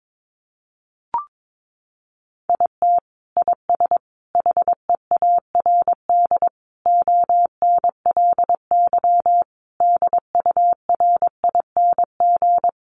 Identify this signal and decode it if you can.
Morse code, heard as a keyed tone, about 22 wpm, 700 Hz. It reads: IT IS HEARD ONLY DURING
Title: Sentences from Top 1200 Words 22wpm